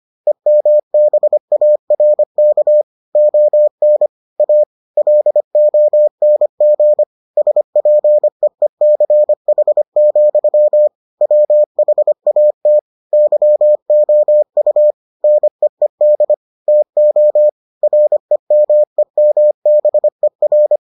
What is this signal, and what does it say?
Morse code, heard as a keyed tone, about 25 wpm, 600 Hz. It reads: EMBARK ON A LONG SPEECH, WHAT YOU NEED TO REMEMBER